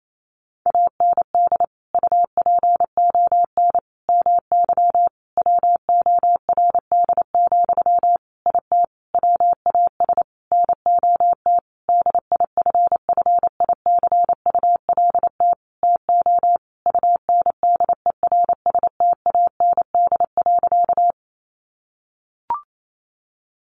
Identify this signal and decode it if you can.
Morse code, heard as a keyed tone, about 28 wpm, 700 Hz. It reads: AND UPON MY WORD, IT WAS NOT DIFFICULT TO UNDERSTAND.
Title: Sentences from Top 1100 Words 28wpm